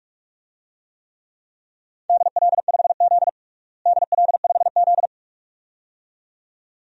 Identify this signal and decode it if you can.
Morse code, heard as a keyed tone, about 46 wpm, 700 Hz. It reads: DL5Z DL5Z
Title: DXCC-100 Call Signs - 4 Characters - Repeated 2x - 45wpm